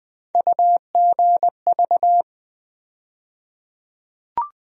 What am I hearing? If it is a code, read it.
Morse code, heard as a keyed tone, about 21 wpm, 700 Hz. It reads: UGV
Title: USA Call Sign Postfixes - 3 Letters 20wpm